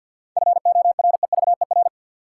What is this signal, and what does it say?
Morse code, heard as a keyed tone, about 53 wpm, 700 Hz. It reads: FOREVER
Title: Sentences from Top 1600 Words - New Word Drill Alt - 50wpm